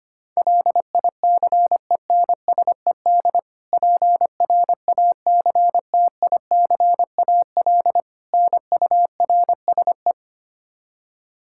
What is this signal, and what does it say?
Morse code, heard as a keyed tone, about 25 wpm, 700 Hz. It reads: LICENSED PRACTICAL NURSE
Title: Practice QSO - Exchange 3 - More Info - 25wpm